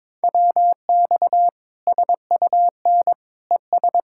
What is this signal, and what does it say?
Morse code, heard as a keyed tone, about 22 wpm, 700 Hz. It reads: WX SUN ES
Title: Practice QSO - Exchange 2 - Info - 22wpm